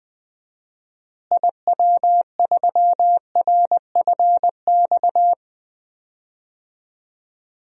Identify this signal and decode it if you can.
Morse code, heard as a keyed tone, about 20 wpm, 700 Hz. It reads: IW3RFX
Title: DXCC-100 Call Signs - 6 Characters - Repeated 2x - 20wpm